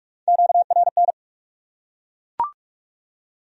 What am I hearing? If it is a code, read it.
Morse code, heard as a keyed tone, about 42 wpm, 700 Hz. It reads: QRN